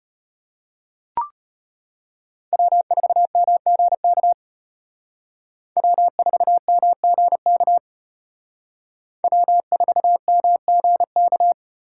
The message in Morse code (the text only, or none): W4MGK W4MGK W4MGK